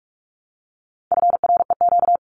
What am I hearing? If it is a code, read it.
Morse code, heard as a keyed tone, about 44 wpm, 700 Hz. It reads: FREQ
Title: Abbreviations Repeated 3x - 45wpm